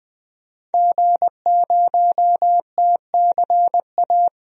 Morse code, heard as a keyed tone, about 21 wpm, 700 Hz. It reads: G0TCA
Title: DXCC-100 Call Signs - 5 Characters - 20wpm